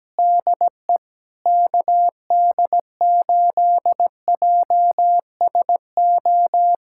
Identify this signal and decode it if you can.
Morse code, heard as a keyed tone, about 17 wpm, 700 Hz. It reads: DE KD8JSO
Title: USA Callsign DE Callsign 17wpm